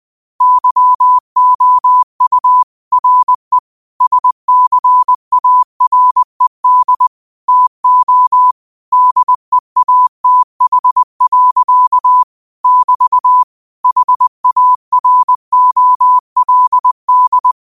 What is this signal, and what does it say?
Morse code, heard as a keyed tone, about 20 wpm, 1000 Hz. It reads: YOU RE SCARED TO DEATH. <BT> HAROLD
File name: qotdfiles / 20wpm